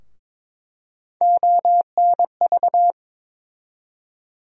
Morse code, heard as a keyed tone, about 22 wpm, 700 Hz. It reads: ONV
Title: USA Call Sign Postfixes - 3 Letters 22wpm